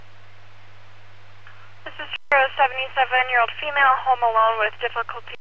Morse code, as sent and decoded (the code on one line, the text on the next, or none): none